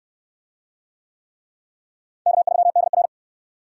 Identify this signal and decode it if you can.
Morse code, heard as a keyed tone, about 66 wpm, 700 Hz. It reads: K3DR